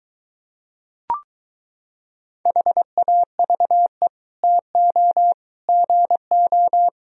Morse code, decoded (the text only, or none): HAVE TO GO